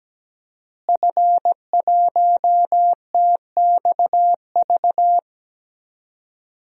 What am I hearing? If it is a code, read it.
Morse code, heard as a keyed tone, about 17 wpm, 700 Hz. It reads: F1TXV
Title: DXCC-100 Call Signs - 5 Characters - Repeated 2x - 17wpm